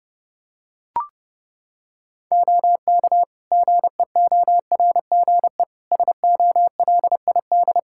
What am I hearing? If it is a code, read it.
Morse code, heard as a keyed tone, about 30 wpm, 700 Hz. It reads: OK GEORGE SOLID
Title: K1USN SST QSO Practice 30wpm